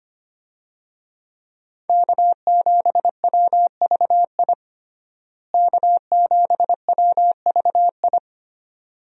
Morse code, heard as a keyed tone, about 25 wpm, 700 Hz. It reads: K7WVI K7WVI
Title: USA Call Signs - 5 Characters - Repeated 2x - 25wpm